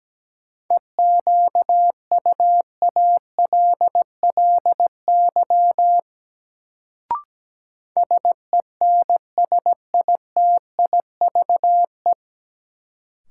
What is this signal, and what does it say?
Morse code, heard as a keyed tone, about 17 wpm, 700 Hz. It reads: EQUALLY SENSITIVE
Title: Sentences from Top 1400 Words - New Word Drill Alt - 17wpm